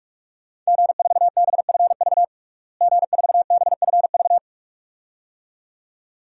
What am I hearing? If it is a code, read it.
Morse code, heard as a keyed tone, about 46 wpm, 700 Hz. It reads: G4BFV G4BFV
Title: DXCC-100 Call Signs - 5 Characters - Repeated 2x - 45wpm